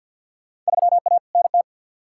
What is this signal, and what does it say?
Morse code, heard as a keyed tone, about 52 wpm, 700 Hz. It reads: 3A NT
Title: Field Day Exchanges Encore - 50wpm